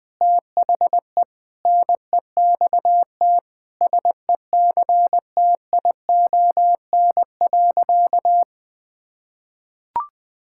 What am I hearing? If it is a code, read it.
Morse code, heard as a keyed tone, about 21 wpm, 700 Hz. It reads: THE NEXT SECTION.
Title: Sentences from Top 1000 Words - Review Part 5 - 20wpm